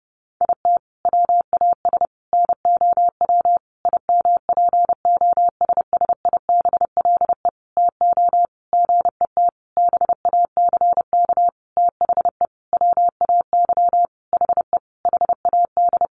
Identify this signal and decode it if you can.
Morse code, heard as a keyed tone, about 30 wpm, 700 Hz. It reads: IT WAS NOW IMPOSSIBLE TO GET BACK THE WAY HE HAD
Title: Sentences from Top 1100 Words 30wpm